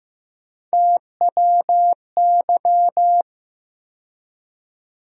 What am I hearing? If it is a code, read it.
Morse code, heard as a keyed tone, about 15 wpm, 700 Hz. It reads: TWY
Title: USA Call Sign Postfixes - 3 Letters 15wpm